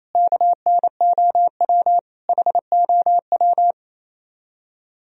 Morse code, heard as a keyed tone, about 35 wpm, 700 Hz. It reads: KNOW HOW